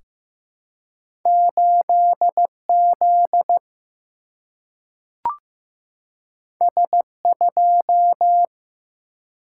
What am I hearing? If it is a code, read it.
Morse code, heard as a keyed tone, about 15 wpm, 700 Hz. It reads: 8Z S2
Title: Single-Letter Single-Number 15wpm